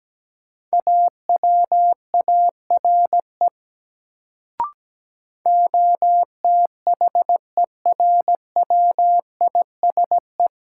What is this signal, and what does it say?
Morse code, heard as a keyed tone, about 17 wpm, 700 Hz. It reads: AWARE OTHERWISE
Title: Sentences from Top 1200 Words - New Word Drill - 17wpm